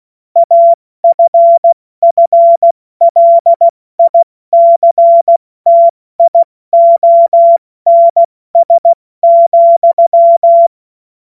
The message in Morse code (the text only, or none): AFFLICTIONS,